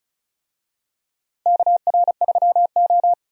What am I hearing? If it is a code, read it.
Morse code, heard as a keyed tone, about 36 wpm, 700 Hz. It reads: KR3O
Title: USA Call Signs - 4 Characters - 35wpm